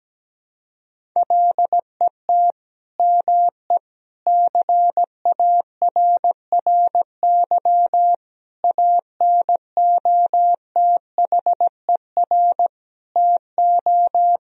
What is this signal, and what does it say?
Morse code, heard as a keyed tone, about 17 wpm, 700 Hz. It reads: LET ME CARRY ANOTHER TO
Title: Sentences from Top 1000 Words - Review Part 1 - 17wpm